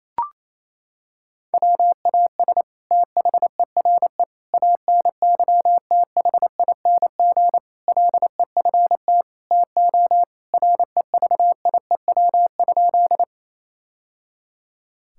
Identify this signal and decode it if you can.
Morse code, heard as a keyed tone, about 28 wpm, 700 Hz. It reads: WAS THERE ANYTHING LEFT TO REVIEW?
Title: Sentences from Top 1000 Words - Review Part 6 - 28wpm